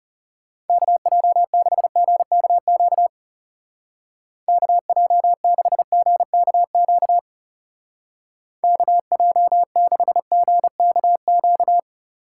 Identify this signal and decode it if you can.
Morse code, heard as a keyed tone, about 36 wpm, 700 Hz. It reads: KJ6GKQ KJ6GKQ KJ6GKQ
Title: Speed Racing - USA 2x3 Call Signs 30wpm